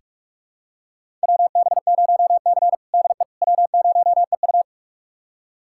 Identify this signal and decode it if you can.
Morse code, heard as a keyed tone, about 46 wpm, 700 Hz. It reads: WB0C DE W0EU
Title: USA Callsign DE Callsign - 4 Characters - 45wpm